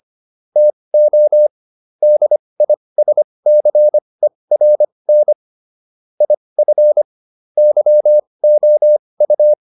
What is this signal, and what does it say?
Morse code, heard as a keyed tone, about 25 wpm, spaced as for 20 wpm, 600 Hz. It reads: TO DISCERN IF YOU